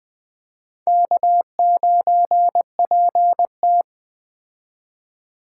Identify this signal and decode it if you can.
Morse code, heard as a keyed tone, about 20 wpm, 700 Hz. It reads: K9PT
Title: Speed Racing - USA 1x2 Call Signs 17wpm